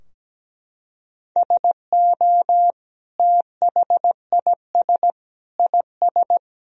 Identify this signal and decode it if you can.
Morse code, heard as a keyed tone, about 17 wpm, 700 Hz. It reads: SO THIS IS
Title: Sentences from Top 1000 Words - Review Part 2 - 17wpm